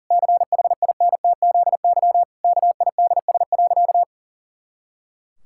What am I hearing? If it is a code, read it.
Morse code, heard as a keyed tone, about 40 wpm, 700 Hz. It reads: CHINTZY KIDS.
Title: Pangram Challenge v2 - 40wpm